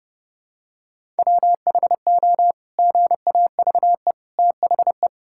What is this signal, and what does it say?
Morse code, heard as a keyed tone, about 32 wpm, 700 Hz. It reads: WHO GAVE THE
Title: Sentences from Top 1000 Words - Review Part 6 - 30wpm